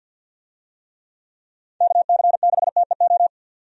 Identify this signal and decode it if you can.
Morse code, heard as a keyed tone, about 51 wpm, 700 Hz. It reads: KC6TEO
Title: USA Call Signs - 6 Characters - 50wpm